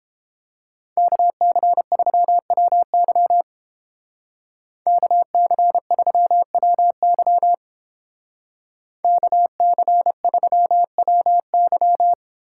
Morse code, heard as a keyed tone, about 28 wpm, 700 Hz. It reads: KC3WY KC3WY KC3WY